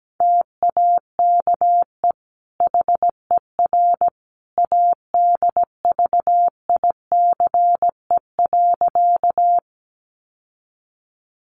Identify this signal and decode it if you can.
Morse code, heard as a keyed tone, about 17 wpm, 700 Hz. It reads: TAKE HER ADVICE.
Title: Sentences from Top 1100 Words 17wpm